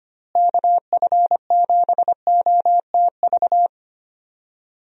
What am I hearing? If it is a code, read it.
Morse code, heard as a keyed tone, about 25 wpm, 700 Hz. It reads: KF7OTV
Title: USA Call Signs - 6 Characters - Repeated 2x - 25wpm